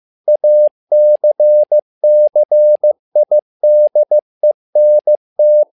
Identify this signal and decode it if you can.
Morse code, heard as a keyed tone, about 15 wpm, 600 Hz. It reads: ACCIDENT
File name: learning center / tower trouble / tower trouble 15wpm